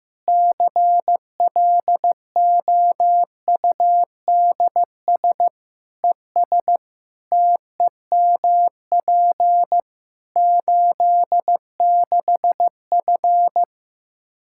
Morse code, heard as a keyed tone, about 16 wpm, 700 Hz. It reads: CLOUDS ES TEMP 86F
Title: Practice QSO - Exchange 2 - Info - 15wpm